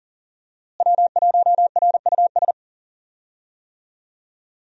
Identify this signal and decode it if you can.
Morse code, heard as a keyed tone, about 38 wpm, 700 Hz. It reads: W1RUS
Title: USA Call Signs - 5 Characters - Repeated 2x - 40wpm